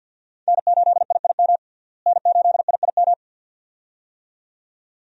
Morse code, heard as a keyed tone, about 51 wpm, 700 Hz. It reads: N8IIM N8IIM
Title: USA Call Signs - 5 Characters - Repeated 2x - 50wpm